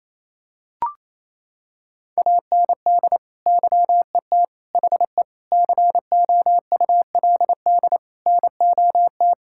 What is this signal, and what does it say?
Morse code, heard as a keyed tone, about 28 wpm, 700 Hz. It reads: AND YET HE COULD NOT
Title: Sentences from Top 1600 Words 28wpm